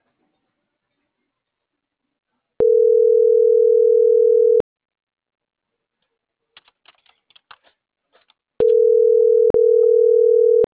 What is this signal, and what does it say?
Ringback tone (North America), 440 + 480 Hz.